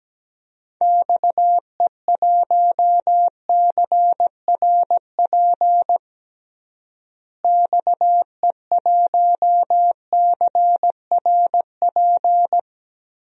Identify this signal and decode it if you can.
Morse code, heard as a keyed tone, about 17 wpm, 700 Hz. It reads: XE1CRP XE1CRP
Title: DXCC-100 Call Signs - 6 Characters - Repeated 3x - 17wpm